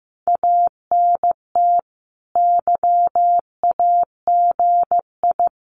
Morse code, heard as a keyed tone, about 15 wpm, 700 Hz. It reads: ANT YAGI
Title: Common QSO Phrases Encore 15wpm